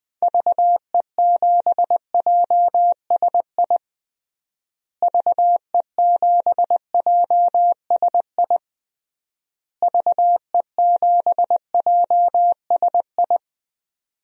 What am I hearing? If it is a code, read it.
Morse code, heard as a keyed tone, about 19 wpm, 700 Hz. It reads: VE7JSI VE7JSI VE7JSI
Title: DXCC-100 Call Signs - 6 Characters - Repeated 3x - 20wpm